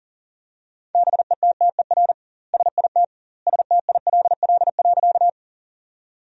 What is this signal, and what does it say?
Morse code, heard as a keyed tone, about 40 wpm, 700 Hz. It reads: BETTER SIT STILL.
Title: 2x Word-Spacing - Sentences from Top 500 Words - 40wpm